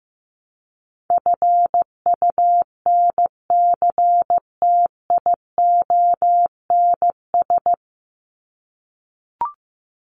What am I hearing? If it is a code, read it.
Morse code, heard as a keyed tone, about 15 wpm, 700 Hz. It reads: FUNCTIONS
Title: ES:S Suffix - Top 100 Words with Base Words 15wpm